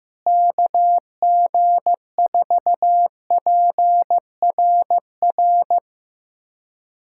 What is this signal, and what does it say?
Morse code, heard as a keyed tone, about 16 wpm, 700 Hz. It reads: KG4PRR
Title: USA Callsign DE Callsign - 6 Characters - 15wpm